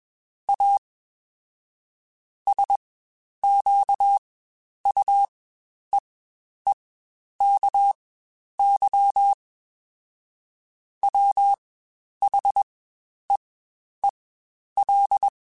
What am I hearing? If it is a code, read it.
Morse code, heard as a keyed tone, about 21 wpm, spaced as for 10 wpm, 800 Hz. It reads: A SQUEEKY WHEEL
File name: Spaced / Chuckles Spaced-mp3s / Squeaky sp @20